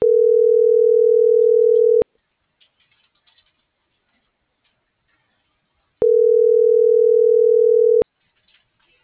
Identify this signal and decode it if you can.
Ringback tone (North America), 440 + 480 Hz.